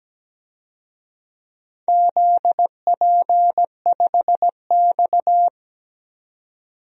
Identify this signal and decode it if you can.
Morse code, heard as a keyed tone, about 17 wpm, 700 Hz. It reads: ZP5X
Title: DXCC-100 Call Signs - 4 Characters - 17wpm